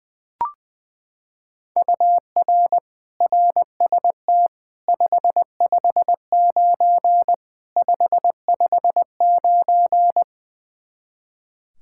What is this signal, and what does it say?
Morse code, heard as a keyed tone, about 20 wpm, 700 Hz. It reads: UR RST 559 559
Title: K1USN SST QSO Practice 20wpm